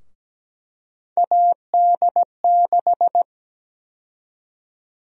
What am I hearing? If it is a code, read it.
Morse code, heard as a keyed tone, about 17 wpm, 700 Hz. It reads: AD6
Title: USA Call Sign Prefixes - 17wpm